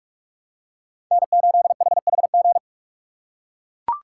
Morse code, heard as a keyed tone, about 42 wpm, 700 Hz. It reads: N8HHG
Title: K1USN SST QSO Practice - 45wpm